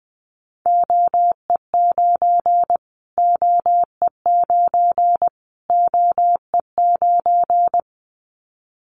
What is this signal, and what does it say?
Morse code, heard as a keyed tone, about 20 wpm, 700 Hz. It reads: OE9 OE9 OE9
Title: Callsign Prefixes for Top 25 DXCC - Repeated 3x - 20wpm